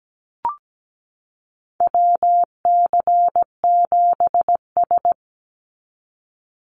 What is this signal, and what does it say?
Morse code, heard as a keyed tone, about 17 wpm, 700 Hz. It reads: WC7S
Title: Call Signs - 17wpm